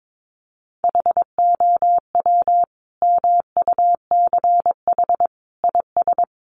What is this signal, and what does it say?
Morse code, heard as a keyed tone, about 22 wpm, 700 Hz. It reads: HOW MUCH IS
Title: Sentences from Top 1200 Words 22wpm